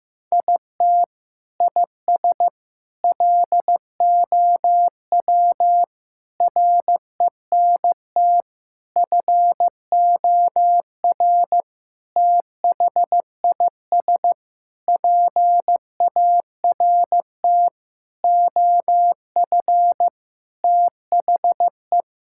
Morse code, heard as a keyed tone, about 15 wpm, 700 Hz. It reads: IT IS LOW RENT FOR THIS PART OF THE
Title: Sentences from Top 1000 Words - Review Part 5 - 15wpm